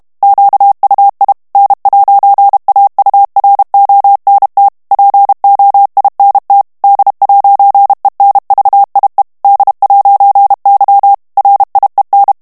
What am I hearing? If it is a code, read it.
Morse code, heard as a keyed tone, about 32 wpm, 800 Hz. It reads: QUI N'AURONT POINT D'ENVIE D'Y RIEN
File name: FRANCAIS TEXTE CLAIR 2 VITESSE 30 MOTS MINUTES